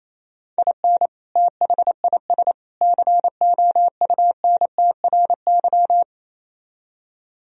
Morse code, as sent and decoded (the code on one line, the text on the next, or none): .. -. / - .... .. ... / -.-. --- ..- -. - .-. -.--
IN THIS COUNTRY